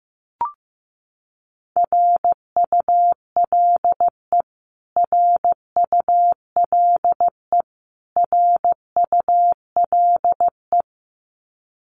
Morse code, heard as a keyed tone, about 15 wpm, 700 Hz. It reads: RULE RULE RULE
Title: Top 500 Words Repeated 15wpm